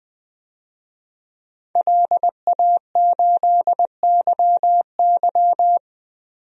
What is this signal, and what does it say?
Morse code, heard as a keyed tone, about 20 wpm, 700 Hz. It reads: LA8YY
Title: DXCC-100 Call Signs - 5 Characters - Repeated 3x - 20wpm